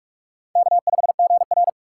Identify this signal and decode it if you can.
Morse code, heard as a keyed tone, about 46 wpm, 700 Hz. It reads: K5GR
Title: USA Call Signs - 4 Characters - 2x Character Spacing - 45wpm